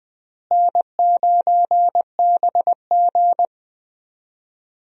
Morse code, heard as a keyed tone, about 20 wpm, 700 Hz. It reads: N9BG
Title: USA Call Signs - 4 Characters - Repeated 3x - 20wpm